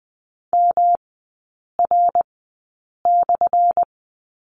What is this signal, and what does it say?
Morse code, heard as a keyed tone, about 20 wpm, spaced as for 8 wpm, 700 Hz. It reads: MR/
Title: 218 Lesson 35e - Number 0 - Three Characters 20wpm